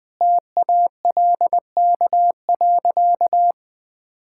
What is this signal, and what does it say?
Morse code, heard as a keyed tone, about 20 wpm, 700 Hz. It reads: TALK.
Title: Sentences from Top 1000 Words - Review Part 6 - 20wpm